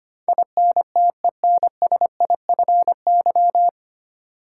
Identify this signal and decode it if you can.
Morse code, heard as a keyed tone, about 25 wpm, 700 Hz. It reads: INTENSIFY
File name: Nine Letter Words - 25wpm